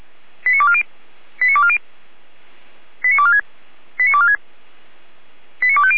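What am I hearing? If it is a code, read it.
Five-tone selective call (ZVEI), heard as quick runs of five tones, tones of about 70 ms. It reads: 89240 89240 89246 89246 89250
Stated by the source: Five-tone ZVEI selective call 89169, received by SDR.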